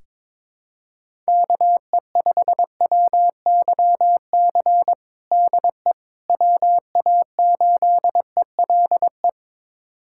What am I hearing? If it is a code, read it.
Morse code, heard as a keyed tone, about 21 wpm, 700 Hz. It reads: KE5WYC DE WA8ELE
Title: USA Callsign DE Callsign - 6 Characters - 22wpm